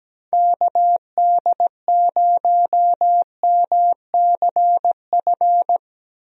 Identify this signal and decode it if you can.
Morse code, heard as a keyed tone, about 17 wpm, 700 Hz. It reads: KD0MCF
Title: USA Call Signs - 6 Characters - Repeated 2x - 17wpm